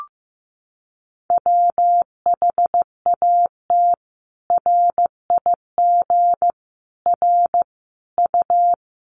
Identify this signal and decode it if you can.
Morse code, heard as a keyed tone, about 15 wpm, 700 Hz. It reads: WHAT RIG R U
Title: Common QSO Phrases 15wpm